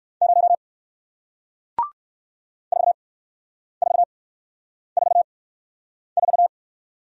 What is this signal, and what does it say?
Morse code, heard as a keyed tone, about 58 wpm, spaced as for 9 wpm, 700 Hz. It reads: 0 4444